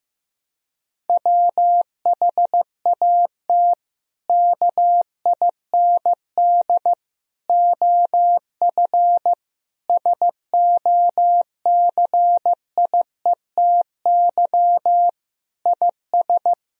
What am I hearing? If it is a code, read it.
Morse code, heard as a keyed tone, about 15 wpm, 700 Hz. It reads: WHAT KIND OF SOCIETY IS
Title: Sentences from Top 1000 Words - Review Part 4 - 15wpm